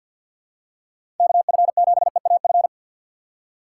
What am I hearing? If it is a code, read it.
Morse code, heard as a keyed tone, about 48 wpm, 700 Hz. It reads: KF7EAF
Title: USA Call Signs - 4x Character Spacing - 50wpm